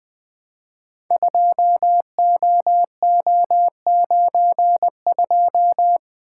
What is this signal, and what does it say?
Morse code, heard as a keyed tone, about 20 wpm, 700 Hz. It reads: 2OO92
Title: Code Groups - 5 Letter:Numbers - 20wpm